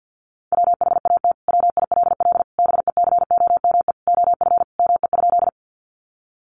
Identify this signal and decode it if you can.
Morse code, heard as a keyed tone, about 50 wpm, 700 Hz. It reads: WHAT WILL BECOME OF ME?
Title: Sentences from Top 600 Words 50wpm